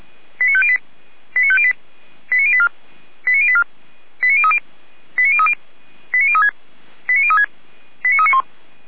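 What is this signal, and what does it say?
Five-tone selective call (ZVEI), heard as quick runs of five tones, tones of about 70 ms. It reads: 89508 89508 89084 89084 89030 89030 89036 89036 89391
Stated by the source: Five-tone ZVEI selective call received by SDR.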